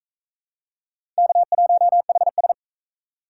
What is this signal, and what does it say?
Morse code, heard as a keyed tone, about 37 wpm, 700 Hz. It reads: K1HS